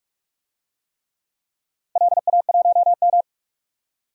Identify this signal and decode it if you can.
Morse code, heard as a keyed tone, about 44 wpm, 700 Hz. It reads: LA1M
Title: DXCC-100 Call Signs - 4 Characters - Repeated 3x - 45wpm